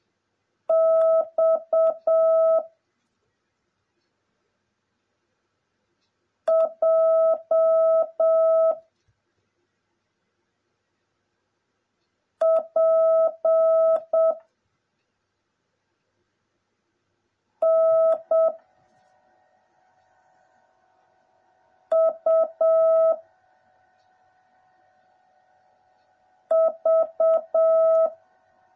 Morse code, heard as a keyed tone, about 7 wpm, 650 Hz. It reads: X J P N U V
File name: jh8cbh / 2021cw / cw09a